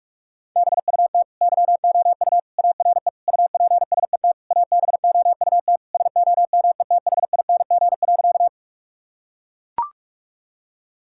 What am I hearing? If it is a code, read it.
Morse code, heard as a keyed tone, about 41 wpm, 700 Hz. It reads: BUT YOU ARE UPSET ABOUT SOMETHING.
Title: Sentences from Top 1000 Words - Review Part 5 - 45wpm